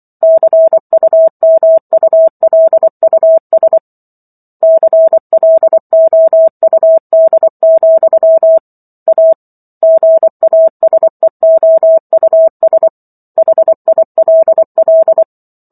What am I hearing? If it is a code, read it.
Morse code, heard as a keyed tone, about 24 wpm, 650 Hz. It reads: CUMULUS CLOUD, A GASEOUS HILL
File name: Chapter 15 - 24 WPM